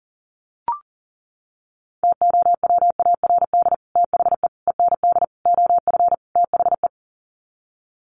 Morse code, heard as a keyed tone, about 40 wpm, 700 Hz. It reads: TOWARD THE END OF THE
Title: Sets of 5 Words 40wpm